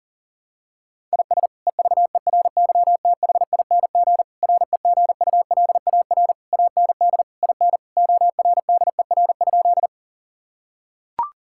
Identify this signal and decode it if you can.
Morse code, heard as a keyed tone, about 40 wpm, 700 Hz. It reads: IS EVERYTHING REGULAR AND IN ORDER?
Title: Sentences from Top 1000 Words - Review Part 6 - 40wpm